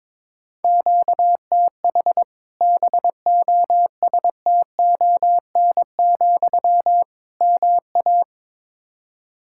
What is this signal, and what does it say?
Morse code, heard as a keyed tone, about 22 wpm, 700 Hz. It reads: QTH BOSTON, MA
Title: Practice QSO - Exchange 1 - Beginning - 22wpm